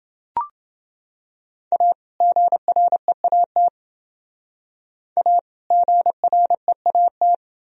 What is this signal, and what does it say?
Morse code, heard as a keyed tone, about 29 wpm, 700 Hz. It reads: A GREAT A GREAT